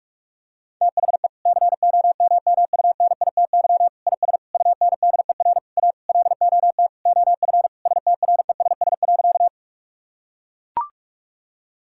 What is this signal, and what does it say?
Morse code, heard as a keyed tone, about 41 wpm, 700 Hz. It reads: THE COMMUNITY IS UNDER A LOT OF STRESS.
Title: Sentences from Top 1000 Words - Review Part 6 - 45wpm